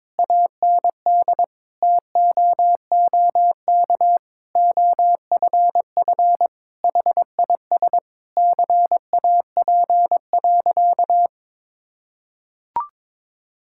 Morse code, heard as a keyed tone, about 22 wpm, 700 Hz. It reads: AND TOOK OFF HIS CAP.
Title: Sentences from Top 1600 Words 22wpm